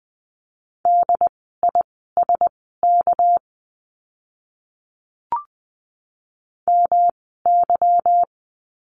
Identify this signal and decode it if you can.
Morse code, heard as a keyed tone, about 20 wpm, spaced as for 15 wpm, 700 Hz. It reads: DISK MY
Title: 199 Lesson 32f - Letter k - Words 20wpm